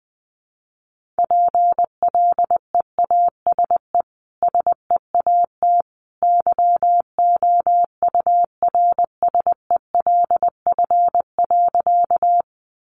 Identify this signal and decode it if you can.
Morse code, heard as a keyed tone, about 20 wpm, 700 Hz. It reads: PLEASE SEAT YOURSELF.
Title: Sentences from Top 1500 Words 20wpm